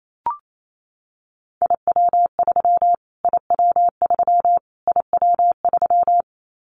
Morse code, heard as a keyed tone, about 28 wpm, 700 Hz. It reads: IW3 IW3 IW3
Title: Callsign Prefixes for Top 25 DXCC - Repeated 3x - 28wpm